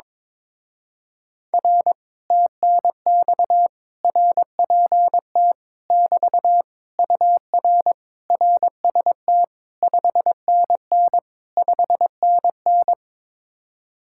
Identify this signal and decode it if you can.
Morse code, heard as a keyed tone, about 22 wpm, 700 Hz. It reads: R TNX RPT <BT> UR RST 5NN 5NN
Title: Speed-Racing Common QSO Phrases - 22wpm